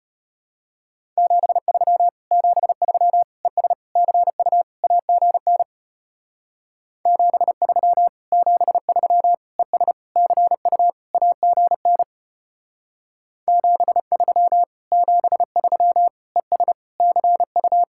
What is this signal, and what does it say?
Morse code, heard as a keyed tone, about 32 wpm, 700 Hz. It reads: 73 73 ES CU AGN 73 73 ES CU AGN 73 73 ES CU